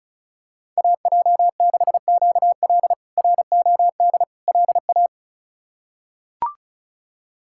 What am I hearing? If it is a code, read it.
Morse code, heard as a keyed tone, about 34 wpm, 700 Hz. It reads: AJ6QL ROD LA
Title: K1USN SST QSO Practice - 35wpm